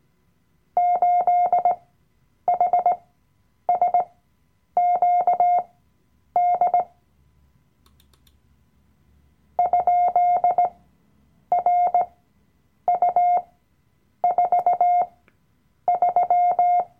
Morse code, heard as a keyed tone, about 18 wpm, spaced as for 8 wpm, 710 Hz. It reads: 8HSQD ?RU43
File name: Workshop Dateien / CW-Online-Kurs / Lektion 007